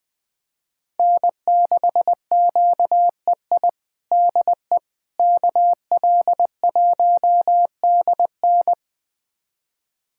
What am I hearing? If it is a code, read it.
Morse code, heard as a keyed tone, about 20 wpm, 700 Hz. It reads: N6QEI DE KL1DN
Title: USA Callsign DE Callsign - 5 Characters - 20wpm